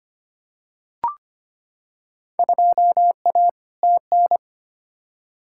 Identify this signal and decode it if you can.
Morse code, heard as a keyed tone, about 25 wpm, 700 Hz. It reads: 2A TN
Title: Field Day Exchanges - 25wpm